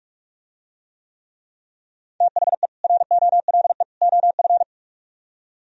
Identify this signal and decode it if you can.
Morse code, heard as a keyed tone, about 47 wpm, 700 Hz. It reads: THE ROLE OF